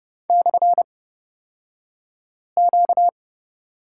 Morse code, heard as a keyed tone, about 30 wpm, spaced as for 5 wpm, 700 Hz. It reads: /Q
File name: Lesson 38d - Letter q - Two Characters 30wpm